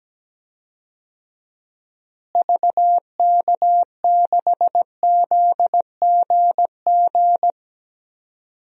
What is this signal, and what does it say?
Morse code, heard as a keyed tone, about 17 wpm, 700 Hz. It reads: VK6ZGG
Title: DXCC-100 Call Signs - 6 Characters - Repeated 2x - 17wpm